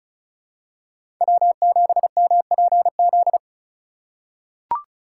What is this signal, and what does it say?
Morse code, heard as a keyed tone, about 34 wpm, 700 Hz. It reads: W7MPZ
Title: USA Call Signs - 5 Characters - 3x Character Spacing - 35wpm